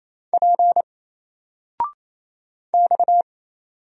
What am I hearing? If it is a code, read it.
Morse code, heard as a keyed tone, about 27 wpm, 700 Hz. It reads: P X
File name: Single-Letter-Number-28wpm